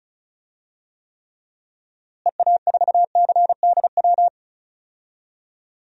Morse code, heard as a keyed tone, about 36 wpm, 700 Hz. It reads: EA4CDW
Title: DXCC-100 Call Signs - 6 Characters - Repeated 3x - 35wpm